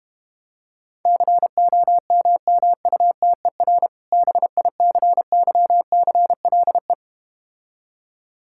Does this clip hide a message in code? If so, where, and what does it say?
Yes: Morse code from 1.1 to 6.9 s, heard as a keyed tone, about 32 wpm, 700 Hz. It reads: COMMUTER BICYCLE